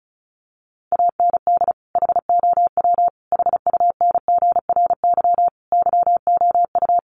Morse code, heard as a keyed tone, about 34 wpm, 700 Hz. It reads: AND HOW HUNGRY YOU
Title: Sentences from Top 1400 Words 35wpm